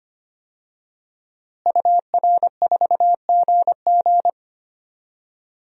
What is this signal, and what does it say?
Morse code, heard as a keyed tone, about 25 wpm, 700 Hz. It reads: UR4GG
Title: DXCC-100 Call Signs - 5 Characters - Repeated 3x - 25wpm